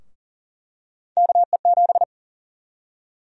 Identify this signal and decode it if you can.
Morse code, heard as a keyed tone, about 40 wpm, 700 Hz. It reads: KE7
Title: USA Call Sign Prefixes - 40wpm